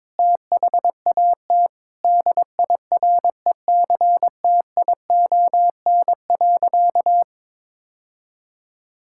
Morse code, heard as a keyed tone, about 22 wpm, 700 Hz. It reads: THAT DIRECTION.
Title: Sentences from Top 1000 Words - Review Part 5 - 22wpm